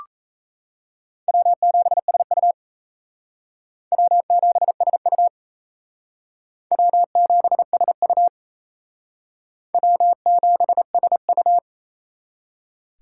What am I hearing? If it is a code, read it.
Morse code, heard as a keyed tone, about 34 wpm, 700 Hz. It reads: W7SU W7SU W7SU W7SU
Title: Speed Racing - USA 1x2 Call Signs 28wpm